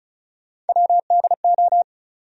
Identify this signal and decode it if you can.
Morse code, heard as a keyed tone, about 36 wpm, 700 Hz. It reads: WDO
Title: USA Call Sign Postfixes - 3 Letters 35wpm